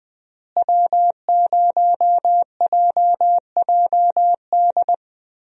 Morse code, heard as a keyed tone, about 21 wpm, 700 Hz. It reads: W0JJD
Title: USA Call Signs - 5 Characters - Repeated 3x - 20wpm